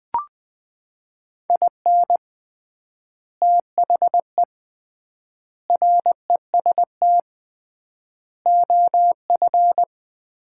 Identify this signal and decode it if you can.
Morse code, heard as a keyed tone, about 20 wpm, 700 Hz. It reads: IN THE REST OF
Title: 3x Word-Spacing - Sets of 5 Words - 20wpm